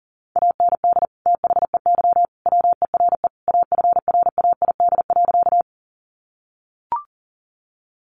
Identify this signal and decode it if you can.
Morse code, heard as a keyed tone, about 40 wpm, 700 Hz. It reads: AND THEY WERE AFRAID.
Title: Sentences from Top 1500 Words 40wpm